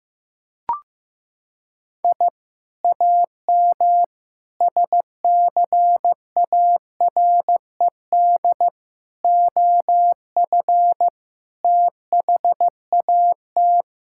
Morse code, heard as a keyed tone, about 15 wpm, 700 Hz. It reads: I AM SCARED OF THAT
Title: Sentences from Top 1000 Words - Review Part 5 - 15wpm